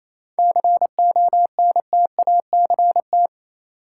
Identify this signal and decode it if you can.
Morse code, heard as a keyed tone, about 28 wpm, 700 Hz. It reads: CONTACT
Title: Sentences from Top 1000 Words - Review Part 7 - New Word Drill - 28wpm